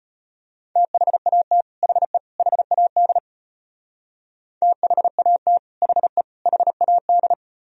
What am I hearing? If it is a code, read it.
Morse code, heard as a keyed tone, about 31 wpm, 700 Hz. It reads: THAT HE HAD THAT HE HAD
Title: Speed-Racing - Sets of 3 Words 25wpm